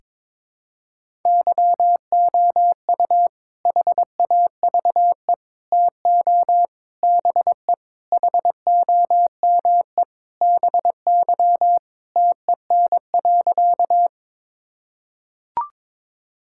Morse code, heard as a keyed tone, about 21 wpm, 700 Hz. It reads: YOU HAVE TO BE HOME BY TEN.
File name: practice / 22wpm / Sentences with 4-Word Phrases - 22wpm